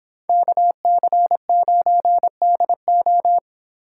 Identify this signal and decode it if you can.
Morse code, heard as a keyed tone, about 25 wpm, 700 Hz. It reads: KC9DO